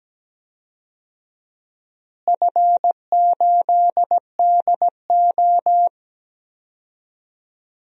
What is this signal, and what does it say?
Morse code, heard as a keyed tone, about 17 wpm, 700 Hz. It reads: F8DO
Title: DXCC-100 Call Signs - 4 Characters - Repeated 3x - 17wpm